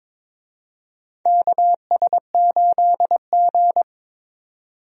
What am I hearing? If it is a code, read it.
Morse code, heard as a keyed tone, about 22 wpm, 700 Hz. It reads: KS8G
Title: USA Call Signs - 4 Characters - 22wpm